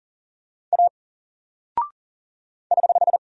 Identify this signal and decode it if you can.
Morse code, heard as a keyed tone, about 40 wpm, 700 Hz. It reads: A <HH>
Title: Single Character (Letter:Number:Punctuation:Prosign) 40wpm